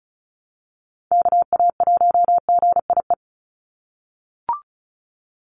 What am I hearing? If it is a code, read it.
Morse code, heard as a keyed tone, about 34 wpm, 700 Hz. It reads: KA1GIE
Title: USA Call Signs 35wpm